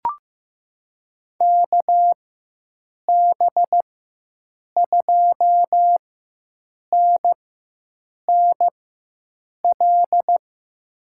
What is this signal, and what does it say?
Morse code, heard as a keyed tone, about 15 wpm, spaced as for 7 wpm, 700 Hz. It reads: KB2NNL
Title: USA Call Signs - 6 Characters - 4x Character Spacing - 15wpm